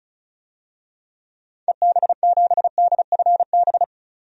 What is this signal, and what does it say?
Morse code, heard as a keyed tone, about 34 wpm, 700 Hz. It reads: EB7DFB
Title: DXCC-100 Call Signs - 6 Characters - 35wpm